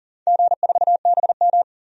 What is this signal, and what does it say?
Morse code, heard as a keyed tone, about 40 wpm, 700 Hz. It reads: G4BM
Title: DXCC-100 Call Signs - 4 Characters - 40wpm